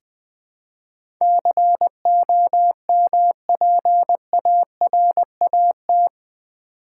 Morse code, heard as a keyed tone, about 20 wpm, 700 Hz. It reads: COMPARAT
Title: Build A Word - 4 Letters At A Time - 11-Letter Words - 20wpm